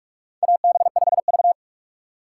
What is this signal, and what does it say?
Morse code, heard as a keyed tone, about 41 wpm, 700 Hz. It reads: AB5V
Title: USA Call Signs - 4 Characters - 4x Character Spacing - 45wpm